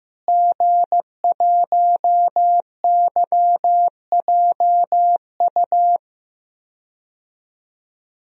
Morse code, heard as a keyed tone, about 15 wpm, 700 Hz. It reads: G1YJU
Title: DXCC-100 Call Signs - 5 Characters - Repeated 3x - 15wpm